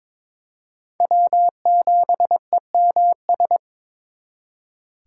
Morse code, heard as a keyed tone, about 26 wpm, 700 Hz. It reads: W7EMS
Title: Speed Racing - USA 1x3 Call Signs 22wpm